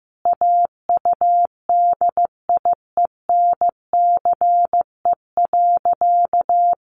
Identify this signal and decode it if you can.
Morse code, heard as a keyed tone, about 15 wpm, 700 Hz. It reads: AUDIENCE.
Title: Sentences from Top 800 Words 15wpm